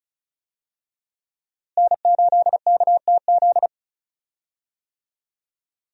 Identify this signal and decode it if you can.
Morse code, heard as a keyed tone, about 36 wpm, 700 Hz. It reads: N8KTZ
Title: USA Call Signs - 5 Characters - Repeated 2x - 35wpm